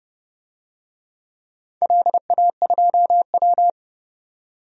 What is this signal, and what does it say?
Morse code, heard as a keyed tone, about 30 wpm, 700 Hz. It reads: LA2W
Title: DXCC-100 Call Signs - 4 Characters - 30wpm